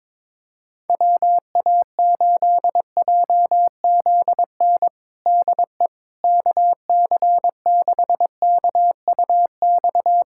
Morse code, heard as a keyed tone, about 22 wpm, 700 Hz. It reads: WA8JZN DE KC6KUX
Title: USA Callsign DE Callsign - 6 Characters - 22wpm